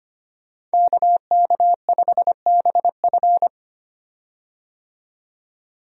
Morse code, heard as a keyed tone, about 25 wpm, 700 Hz. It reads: KK5BF
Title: USA Call Signs - 5 Characters - 25wpm